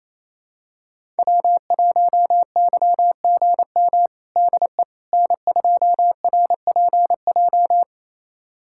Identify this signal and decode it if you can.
Morse code, heard as a keyed tone, about 28 wpm, 700 Hz. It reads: W1YGM DE N2RPJ
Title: USA Callsign DE Callsign - 5 Characters - 28wpm